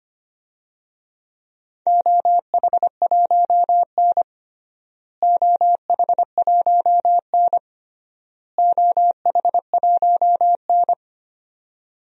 Morse code, heard as a keyed tone, about 25 wpm, 700 Hz. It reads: OH1N OH1N OH1N
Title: DXCC-100 Call Signs - 4 Characters - Repeated 3x - 25wpm